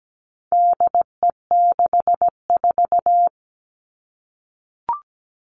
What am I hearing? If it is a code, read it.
Morse code, heard as a keyed tone, about 17 wpm, 700 Hz. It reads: DE64
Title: All Cut Numbers - 4-Digit 17wpm